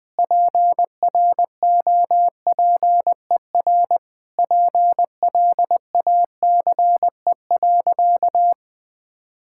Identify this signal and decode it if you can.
Morse code, heard as a keyed tone, about 21 wpm, 700 Hz. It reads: PROPER PLACE.
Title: Sentences from Top 1000 Words - Review Part 5 - 20wpm